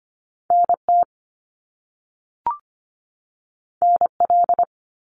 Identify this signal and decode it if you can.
Morse code, heard as a keyed tone, about 25 wpm, 700 Hz. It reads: NT NL
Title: Canadian Provinces and Territories 25wpm